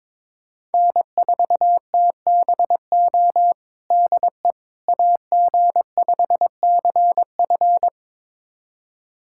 Morse code, heard as a keyed tone, about 22 wpm, 700 Hz. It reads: N4TBO DE AG5CF
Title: USA Callsign DE Callsign - 5 Characters - 22wpm